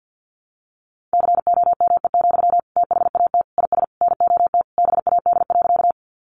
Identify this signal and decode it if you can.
Morse code, heard as a keyed tone, about 50 wpm, 700 Hz. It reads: COME, THAT IS NOT BAD.
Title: Sentences from Top 700 Words 50wpm